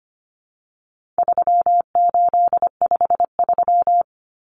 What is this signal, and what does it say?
Morse code, heard as a keyed tone, about 25 wpm, 700 Hz. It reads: 3853
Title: Common Cut Numbers - 4-Digit 25wpm